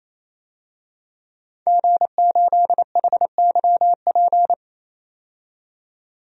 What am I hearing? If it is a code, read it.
Morse code, heard as a keyed tone, about 28 wpm, 700 Hz. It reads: G8HYP
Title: DXCC-100 Call Signs - 5 Characters - 28wpm